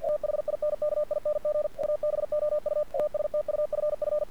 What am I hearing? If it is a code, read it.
Morse code, heard as a keyed tone, about 50 wpm, 600 Hz. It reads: THINKING ABOUT STUFF
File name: archived / CRSnet